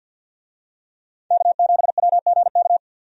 Morse code, heard as a keyed tone, about 51 wpm, 700 Hz. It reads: K7WGK
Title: USA Call Signs - 5 Characters - 50wpm